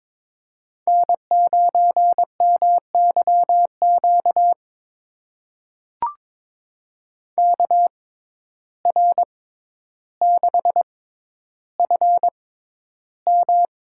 Morse code, heard as a keyed tone, about 22 wpm, spaced as for 11 wpm, 700 Hz. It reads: N9MYQ KR6FM
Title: USA Call Signs - 5 Characters - 6x Character Spacing - 22wpm